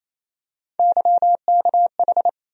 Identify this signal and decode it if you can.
Morse code, heard as a keyed tone, about 28 wpm, 700 Hz. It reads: YKH
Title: USA Call Sign Postfixes - 3 Letters 28wpm